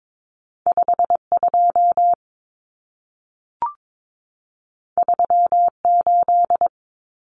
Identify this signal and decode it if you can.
Morse code, heard as a keyed tone, about 22 wpm, 700 Hz. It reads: 52 38
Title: Numbers - 2-Digit 22wpm